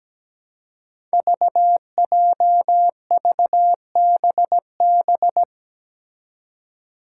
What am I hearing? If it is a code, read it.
Morse code, heard as a keyed tone, about 17 wpm, 700 Hz. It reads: VJVBB
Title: Five Challenging Letter-Number Combinations 17wpm